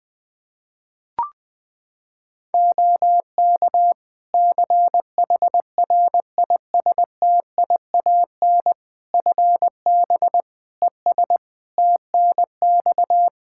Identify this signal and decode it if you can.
Morse code, heard as a keyed tone, about 20 wpm, 700 Hz. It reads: OK CHRISTIAN FB ES TNX
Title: Practice QSO - Exchange 2 - Info - 20wpm